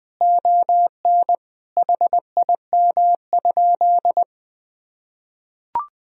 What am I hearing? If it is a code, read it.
Morse code, heard as a keyed tone, about 21 wpm, 700 Hz. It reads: ON HIM?
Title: Sentences from Top 1000 Words - Review Part 2 - 20wpm